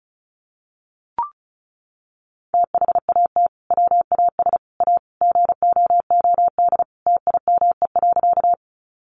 Morse code, heard as a keyed tone, about 34 wpm, 700 Hz. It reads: THAT WAS A GOOD TIME.
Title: Sentences from Top 200 Words 35wpm